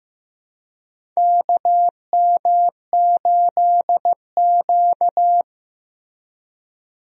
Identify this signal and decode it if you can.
Morse code, heard as a keyed tone, about 15 wpm, 700 Hz. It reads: KM8Q
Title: USA Call Signs - 4 Characters - Repeated 2x - 15wpm